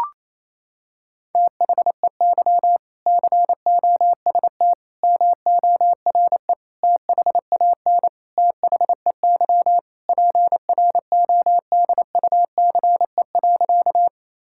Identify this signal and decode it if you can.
Morse code, heard as a keyed tone, about 28 wpm, 700 Hz. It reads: THEY COST MORE THAN THEY PRODUCE.
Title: Sentences from Top 1000 Words - Review Part 5 - 28wpm